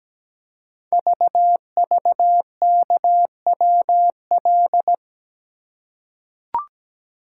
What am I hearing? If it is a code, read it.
Morse code, heard as a keyed tone, about 17 wpm, 700 Hz. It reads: VVKWL
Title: Code Groups - 5 Letters - 17wpm